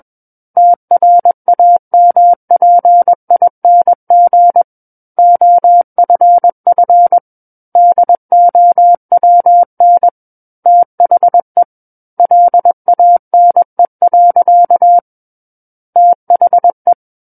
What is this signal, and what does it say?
Morse code, heard as a keyed tone, about 21 wpm, 700 Hz. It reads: TRAMPING OFF DOWN THE LANE. THE
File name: Book One / 12-Chapter 12 - 21 WPM